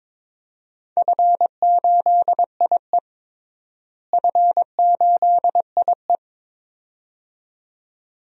Morse code, heard as a keyed tone, about 22 wpm, 700 Hz. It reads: F8IE F8IE
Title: DXCC-100 Call Signs - 4 Characters - Repeated 2x - 22wpm